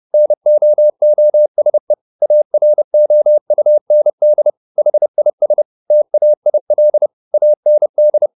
Morse code, heard as a keyed tone, about 30 wpm, 600 Hz. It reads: NOOSE AROUND HIS TAIL AND